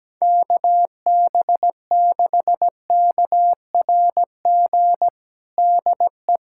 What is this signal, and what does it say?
Morse code, heard as a keyed tone, about 17 wpm, 700 Hz. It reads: KB6KRG DE
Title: USA Callsign DE Callsign - 6 Characters - 17wpm